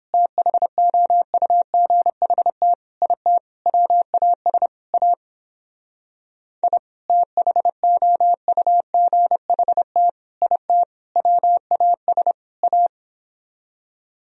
Morse code, heard as a keyed tone, about 29 wpm, 700 Hz. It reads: THOUGHT IT WAS A I THOUGHT IT WAS A